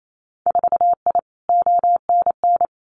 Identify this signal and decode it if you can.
Morse code, heard as a keyed tone, about 28 wpm, 700 Hz. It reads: 4I ONN
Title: Winter Field Day Exchanges Encore 28wpm